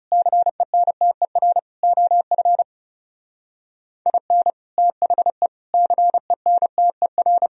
Morse code, heard as a keyed tone, about 39 wpm, 700 Hz. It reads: CENTER OF IN THE CENTER